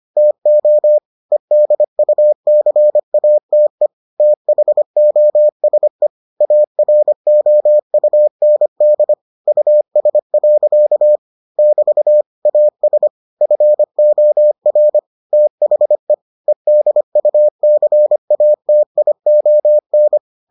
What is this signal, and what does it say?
Morse code, heard as a keyed tone, about 25 wpm, 600 Hz. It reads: TO EDUCATE THOSE AROUND US. <BT> AS FOR THE EDUCATION